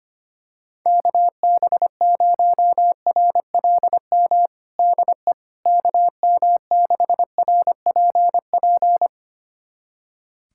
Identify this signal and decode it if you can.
Morse code, heard as a keyed tone, about 25 wpm, 700 Hz. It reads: KB0RLM DE KM6RPP
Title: USA Callsign DE Callsign - 6 Characters - 25wpm